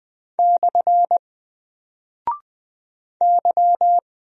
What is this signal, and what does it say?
Morse code, heard as a keyed tone, about 20 wpm, 700 Hz. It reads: / Y
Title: Single Character (Letter:Number:Punctuation:Prosign) 20wpm